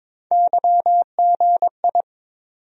Morse code, heard as a keyed tone, about 22 wpm, 700 Hz. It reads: YGI
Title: USA Call Sign Postfixes - 3 Letters 22wpm